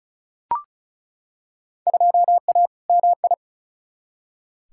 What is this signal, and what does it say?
Morse code, heard as a keyed tone, about 36 wpm, 700 Hz. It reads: 2A MI
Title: Field Day Exchanges - 35wpm